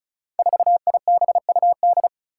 Morse code, heard as a keyed tone, about 36 wpm, 700 Hz. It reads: 4IBUD
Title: Five Challenging Letter-Number Combinations 35wpm